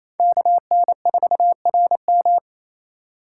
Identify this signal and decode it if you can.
Morse code, heard as a keyed tone, about 28 wpm, 700 Hz. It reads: KN4RM
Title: USA Call Signs - 3x Character Spacing - 28wpm